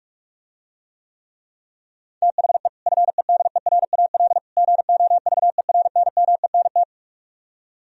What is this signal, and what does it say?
Morse code, heard as a keyed tone, about 44 wpm, 700 Hz. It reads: THE FEDERAL GOVERNMENT